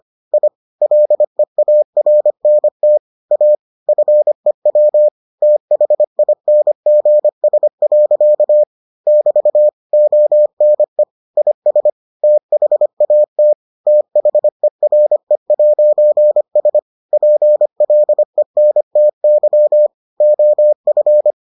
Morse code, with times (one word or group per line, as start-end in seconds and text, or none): 0.3-0.5 I
0.8-3.0 LEARNT
3.3-3.5 A
3.9-5.1 FEW
5.4-8.6 THINGS.
9.1-9.7 <BT>
9.9-11.0 ONE
11.4-11.9 IS
12.2-13.5 THAT
13.9-16.8 THERE'S
17.1-19.9 PLENTY
20.2-21.3 OF